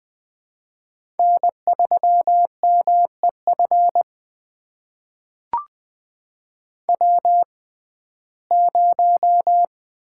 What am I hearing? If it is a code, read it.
Morse code, heard as a keyed tone, about 20 wpm, spaced as for 14 wpm, 700 Hz. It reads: N3MEF W0
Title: USA Call Signs - 5 Characters - 6x Character Spacing - 20wpm